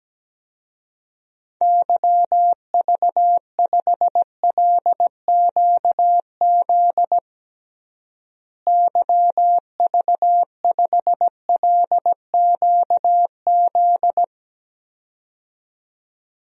Morse code, heard as a keyed tone, about 17 wpm, 700 Hz. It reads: YV5LQZ YV5LQZ
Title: DXCC-100 Call Signs - 6 Characters - Repeated 2x - 17wpm